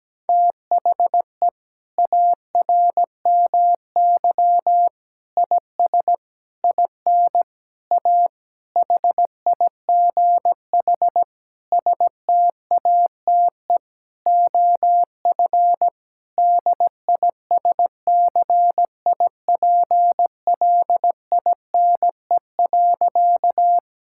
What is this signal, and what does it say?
Morse code, heard as a keyed tone, about 17 wpm, 700 Hz. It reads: THE ARMY IS IN A HIGH STATE OF DISCIPLINE.
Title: Sentences from Top 1000 Words - Review Part 6 - 17wpm